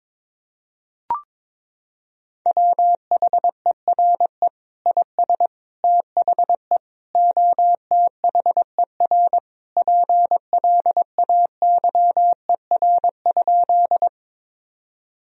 Morse code, heard as a keyed tone, about 21 wpm, 700 Hz. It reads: WHERE IS THE OTHER PLAYER?
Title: Sentences from Top 1000 Words - Review Part 4 - 22wpm